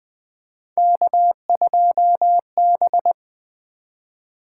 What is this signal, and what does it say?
Morse code, heard as a keyed tone, about 20 wpm, 700 Hz. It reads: K2B
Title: Three Letter-Number Combinations - 20wpm